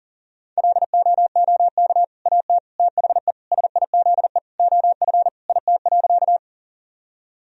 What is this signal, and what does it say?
Morse code, heard as a keyed tone, about 40 wpm, 700 Hz. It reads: LOOK AT THE SIZE OF IT.
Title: Sentences from Top 1000 Words - Review Part 2 - 40wpm